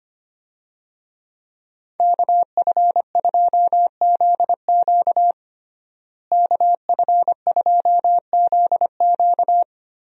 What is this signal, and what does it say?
Morse code, heard as a keyed tone, about 25 wpm, 700 Hz. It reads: KF2ZQ KF2ZQ
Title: USA Call Signs - 5 Characters - Repeated 3x - 25wpm